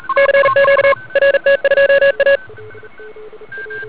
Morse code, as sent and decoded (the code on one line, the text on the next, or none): -.-. --.- / .-. - ..--- .-
CQ RT2A